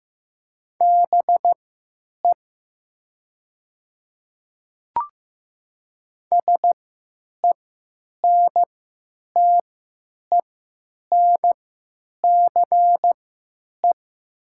Morse code, heard as a keyed tone, about 15 wpm, spaced as for 9 wpm, 700 Hz. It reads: BE SENTENCE
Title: 3x Character Spacing - Top 200 Words - 15wpm